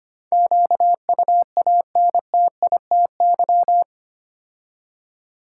Morse code, heard as a keyed tone, about 25 wpm, 700 Hz. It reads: QUANTITY
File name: 8 Letter Words - First 3 Letters Given - 25wpm